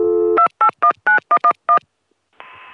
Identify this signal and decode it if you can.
DTMF, heard as telephone touch tones, tones of about 90 ms. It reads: 2819111